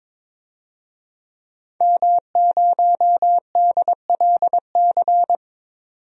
Morse code, heard as a keyed tone, about 22 wpm, 700 Hz. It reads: M0DLC
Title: DXCC-100 Call Signs - 5 Characters - 22wpm